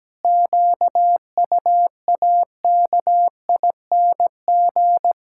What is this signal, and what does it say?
Morse code, heard as a keyed tone, about 17 wpm, 700 Hz. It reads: QUAKING
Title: Pangram Challenge v2 - 17wpm